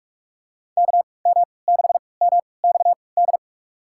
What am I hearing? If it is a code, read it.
Morse code, heard as a keyed tone, about 46 wpm, spaced as for 26 wpm, 700 Hz. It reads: KM6MXD
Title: USA Call Signs - 6 Characters - 3x Character Spacing - 45wpm